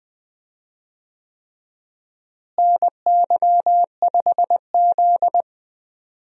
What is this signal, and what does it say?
Morse code, heard as a keyed tone, about 20 wpm, 700 Hz. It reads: NY5Z
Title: USA Call Signs - 4 Characters - Repeated 3x - 20wpm